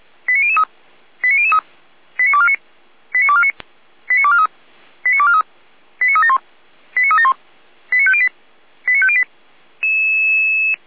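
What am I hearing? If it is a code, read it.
Five-tone selective call (ZVEI), heard as quick runs of five tones, tones of about 70 ms. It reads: 89003 89003 89249 89249 89243 89243 89371 89371 89508 89508
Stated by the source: ZVEI five-tone call received by SDR, from feuerwehr-orlamuende.org archive.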